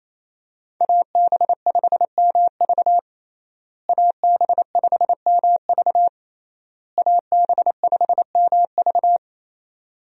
Morse code, heard as a keyed tone, about 28 wpm, 700 Hz. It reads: AB5MV AB5MV AB5MV
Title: USA Call Signs - 5 Characters - Repeated 3x - 28wpm